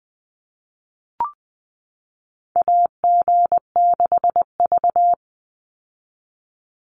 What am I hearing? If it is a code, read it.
Morse code, heard as a keyed tone, about 20 wpm, 700 Hz. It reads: AG6V
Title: Call Signs - 20wpm